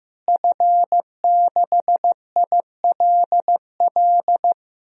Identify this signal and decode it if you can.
Morse code, heard as a keyed tone, about 15 wpm, 700 Hz. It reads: F6ILL
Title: DXCC-100 Call Signs - 5 Characters - Repeated 2x - 15wpm